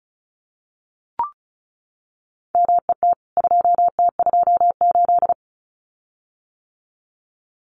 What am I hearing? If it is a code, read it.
Morse code, heard as a keyed tone, about 34 wpm, 700 Hz. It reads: MET 2T28
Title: CWT Exchange 35wpm